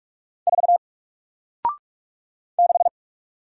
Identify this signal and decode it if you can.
Morse code, heard as a keyed tone, about 46 wpm, 700 Hz. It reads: 4 6
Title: Single Character (Letter:Number:Punctuation:Prosign) 45wpm